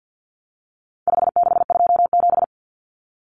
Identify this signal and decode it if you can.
Morse code, heard as a keyed tone, about 50 wpm, 700 Hz. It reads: 5627
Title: Common Cut Numbers - 4-Digit 50wpm